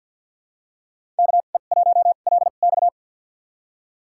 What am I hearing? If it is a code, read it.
Morse code, heard as a keyed tone, about 52 wpm, spaced as for 37 wpm, 700 Hz. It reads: KE1LX
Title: USA Call Signs - 2x Character Spacing - 50wpm